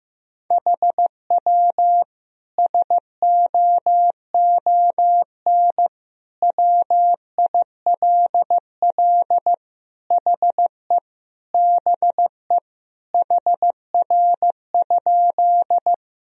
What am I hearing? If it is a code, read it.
Morse code, heard as a keyed tone, about 15 wpm, 700 Hz. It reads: HW SOON WILL HE BE HR?
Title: Sentences from Top 400 Words and Abbreviations 15wpm